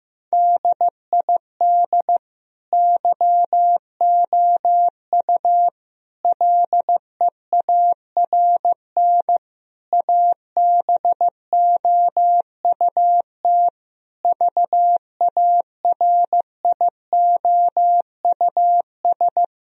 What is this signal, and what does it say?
Morse code, heard as a keyed tone, about 14 wpm, 700 Hz. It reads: DID YOU LEARN ABOUT VARIOUS
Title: Sentences from Top 1000 Words - Review Part 4 - 15wpm